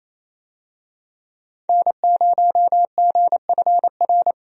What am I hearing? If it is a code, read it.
Morse code, heard as a keyed tone, about 28 wpm, 700 Hz. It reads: N0GFR
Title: USA Call Signs - 5 Characters - 28wpm